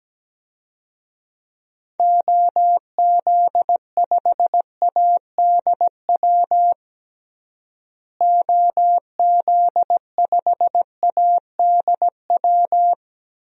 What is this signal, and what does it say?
Morse code, heard as a keyed tone, about 17 wpm, 700 Hz. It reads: OZ5ADW OZ5ADW
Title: DXCC-100 Call Signs - 6 Characters - Repeated 3x - 17wpm